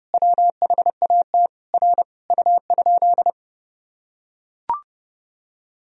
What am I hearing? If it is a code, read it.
Morse code, heard as a keyed tone, about 32 wpm, 700 Hz. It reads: WHAT R U?
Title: Sentences from Top 100 Words and Abbreviations 30wpm